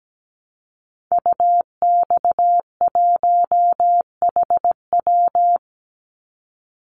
Morse code, heard as a keyed tone, about 17 wpm, 700 Hz. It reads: UX1HW
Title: Call Signs Repeated 2x - 17wpm